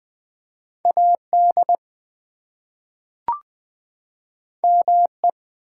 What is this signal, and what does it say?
Morse code, heard as a keyed tone, about 20 wpm, 700 Hz. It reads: AD ME
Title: Two Letter Words - 20wpm